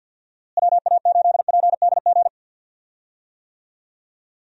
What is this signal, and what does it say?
Morse code, heard as a keyed tone, about 52 wpm, 700 Hz. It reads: WA8PDG
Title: USA Call Signs - 6 Characters - Repeated 3x - 50wpm